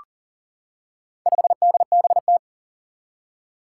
Morse code, heard as a keyed tone, about 40 wpm, 700 Hz. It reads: 5DBT
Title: Four Challenging Letter-Number Combinations 40wpm